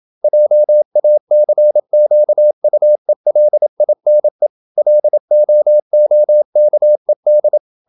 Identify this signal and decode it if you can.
Morse code, heard as a keyed tone, about 27 wpm, 600 Hz. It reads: JACQUELINE LOOKED
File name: Never 0027wpm